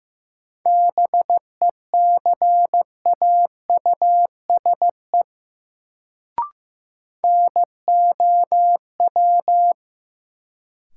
Morse code, heard as a keyed tone, about 15 wpm, 700 Hz. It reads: BECAUSE NOW
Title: Sets of 2 Words - New Word Drill - 15wpm